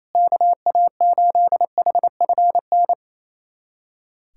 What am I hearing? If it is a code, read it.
Morse code, heard as a keyed tone, about 28 wpm, 700 Hz. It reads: KA8HFN
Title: USA Callsign DE Callsign - 6 Characters - 28wpm